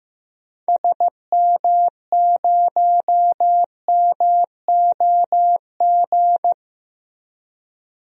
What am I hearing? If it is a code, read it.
Morse code, heard as a keyed tone, about 15 wpm, 700 Hz. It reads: SM0MOG
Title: DXCC-100 Call Signs - 6 Characters - Repeated 2x - 15wpm